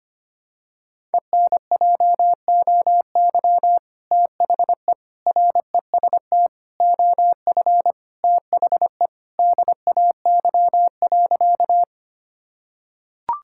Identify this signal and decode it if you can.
Morse code, heard as a keyed tone, about 25 wpm, 700 Hz. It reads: ENJOY THE REST OF THE DAY.
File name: practice / 25wpm / Sentences with 2-Word Phrases - 25wpm